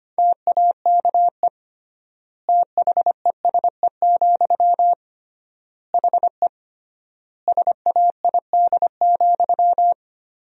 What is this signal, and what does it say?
Morse code, heard as a keyed tone, about 25 wpm, 700 Hz. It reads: TAKE THESE, HE SAID,
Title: 3x Word-Spacing - Sentences from Top 300 Words - 25wpm